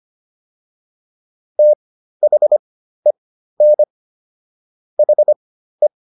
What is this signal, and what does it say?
Morse code, heard as a keyed tone, about 25 wpm, spaced as for 13 wpm, 600 Hz. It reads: THEN HE